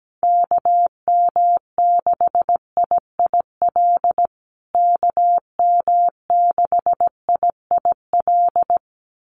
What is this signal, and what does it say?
Morse code, heard as a keyed tone, about 17 wpm, 700 Hz. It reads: KM6IIL KM6IIL
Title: USA Call Signs Repeated 2x 17wpm